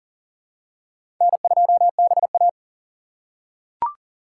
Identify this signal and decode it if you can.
Morse code, heard as a keyed tone, about 39 wpm, 690 Hz. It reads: N2BA
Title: USA Call Signs - 4 Characters - 2x Character Spacing - 40wpm